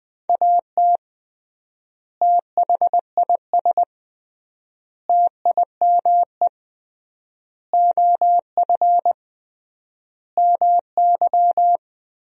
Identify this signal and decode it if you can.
Morse code, heard as a keyed tone, about 20 wpm, 700 Hz. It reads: AT THIS TIME OF MY
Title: 3x Word-Spacing - Sentences from Top 300 Words - 20wpm